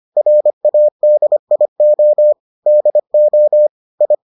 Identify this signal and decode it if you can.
Morse code, heard as a keyed tone, about 25 wpm, 600 Hz. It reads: RADIO DO I